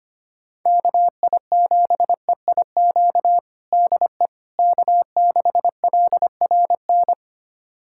Morse code, heard as a keyed tone, about 25 wpm, 700 Hz. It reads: KI7EIQ DE K6LRN
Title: USA Callsign DE Callsign 25wpm